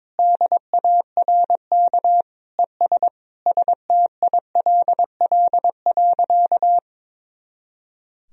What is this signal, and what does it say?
Morse code, heard as a keyed tone, about 22 wpm, 700 Hz. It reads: DARK ES STILL.
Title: Sentences from Top 500 Words and Abbreviations 22wpm